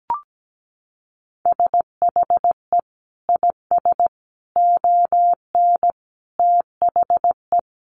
Morse code, heard as a keyed tone, about 17 wpm, 700 Hz. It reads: SHE IS ON THE
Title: Sentences from Top 1500 Words 17wpm